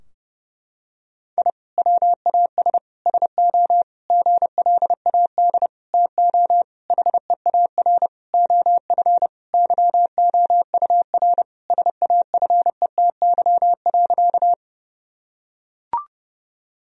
Morse code, heard as a keyed tone, about 30 wpm, 700 Hz. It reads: I WAS SO GLAD TO HEAR OF YOUR SAFETY.
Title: Sentences from Top 1000 Words - Review Part 4 - 30wpm